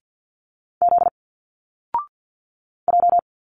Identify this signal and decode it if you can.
Morse code, heard as a keyed tone, about 50 wpm, 700 Hz. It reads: Z J
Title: Single Letters 50wpm